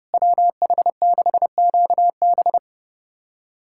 Morse code, heard as a keyed tone, about 30 wpm, 700 Hz. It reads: WH6QB
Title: USA Callsign DE Callsign - 5 Characters - 30wpm